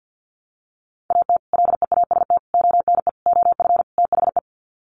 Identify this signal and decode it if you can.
Morse code, heard as a keyed tone, about 50 wpm, 700 Hz. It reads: AT LEAST ONE OF THE
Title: Sets of 5 Words Encore 50wpm